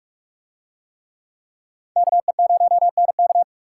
Morse code, heard as a keyed tone, about 44 wpm, 700 Hz. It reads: KE0NK